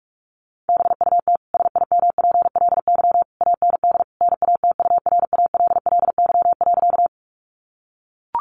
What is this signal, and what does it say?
Morse code, heard as a keyed tone, about 44 wpm, 700 Hz. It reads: BUT SIMPLY AND NATURALLY.
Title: Sentences from Top 1300 Words 45wpm